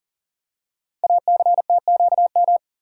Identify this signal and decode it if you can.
Morse code, heard as a keyed tone, about 42 wpm, 700 Hz. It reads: ACTQM
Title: Code Groups - 5 Letters - 40wpm